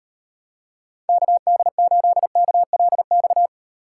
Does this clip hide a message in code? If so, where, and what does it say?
Yes: Morse code from 1.1 to 3.5 s, heard as a keyed tone, about 32 wpm, 700 Hz. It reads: KD8KLX